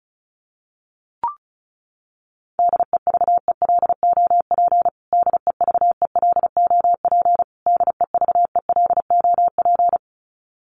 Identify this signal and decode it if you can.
Morse code, heard as a keyed tone, about 34 wpm, 700 Hz. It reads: DEVELOP DEVELOP DEVELOP
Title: Top 500 Words Repeated 35wpm